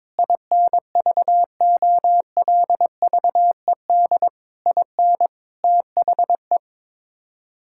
Morse code, heard as a keyed tone, about 22 wpm, 700 Hz. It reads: INVOLVED IN THE
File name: practice / 22wpm / Sets of 6 Words - 22wpm